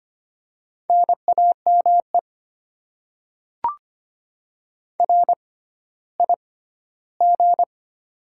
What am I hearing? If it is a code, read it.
Morse code, heard as a keyed tone, about 25 wpm, spaced as for 14 wpm, 700 Hz. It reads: NAME RIG
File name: Top 20 QSO Elements - 6x Character Spacing - 25wpm